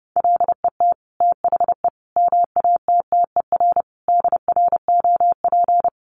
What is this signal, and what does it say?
Morse code, heard as a keyed tone, about 28 wpm, 700 Hz. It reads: LET THE MATTER DROP
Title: Sentences from Top 1400 Words 30wpm